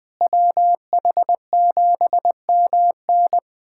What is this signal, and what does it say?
Morse code, heard as a keyed tone, about 20 wpm, 700 Hz. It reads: WH7MN
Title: USA Callsign DE Callsign - 5 Characters - 20wpm